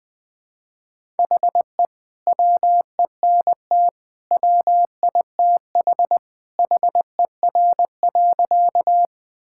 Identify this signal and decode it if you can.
Morse code, heard as a keyed tone, about 21 wpm, 700 Hz. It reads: HE WENT WITH HER.
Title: Sentences with 2-Word Phrases - Spoken Beginning - 20wpm